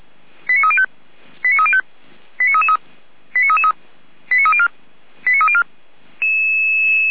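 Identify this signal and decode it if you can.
Five-tone selective call (ZVEI), heard as quick runs of five tones, tones of about 70 ms. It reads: 89305 89305 89333 89304 89304
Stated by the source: ZVEI five-tone selective call received on an SDR.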